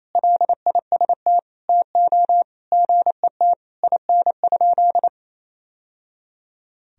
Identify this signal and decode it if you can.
Morse code, heard as a keyed tone, about 28 wpm, 700 Hz. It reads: LIST TO GET IN?
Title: Sentences from Top 1000 Words - Review Part 2 - 28wpm